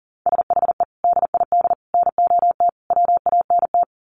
Morse code, heard as a keyed tone, about 40 wpm, 700 Hz. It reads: SHE DID NOT WANT
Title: Sets of 4 Words 40wpm